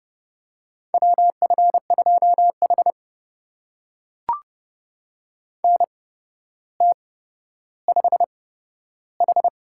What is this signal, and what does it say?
Morse code, heard as a keyed tone, about 28 wpm, spaced as for 13 wpm, 700 Hz. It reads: WF2H NT5H
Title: USA Call Signs - 4 Characters - 8x Character Spacing - 30wpm